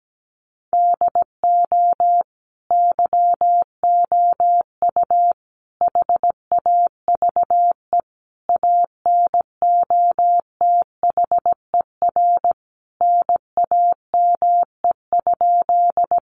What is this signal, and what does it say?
Morse code, heard as a keyed tone, about 17 wpm, 700 Hz. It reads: DO YOU HAVE ANOTHER NAME?
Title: Sentences from Top 600 Words 17wpm